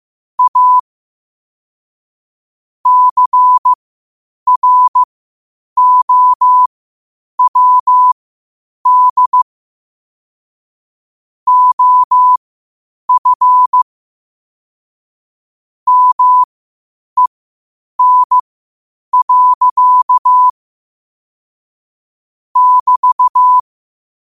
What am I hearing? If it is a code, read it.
Morse code, heard as a keyed tone, about 15 wpm, spaced as for 8 wpm, 1000 Hz. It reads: A CROWD OF MEN. <BT>
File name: qotdfiles / 8wpm